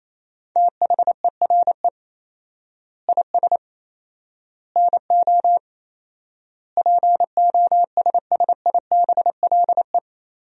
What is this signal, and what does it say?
Morse code, heard as a keyed tone, about 28 wpm, 700 Hz. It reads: THERE IS NO POSSIBLE
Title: 4x Word-Spacing - Sentences from Top 500 Words - 28wpm